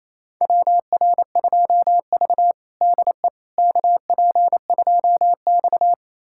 Morse code, heard as a keyed tone, about 28 wpm, 700 Hz. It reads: WR2V DE KP2X
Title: USA Callsign DE Callsign - 4 Characters - 28wpm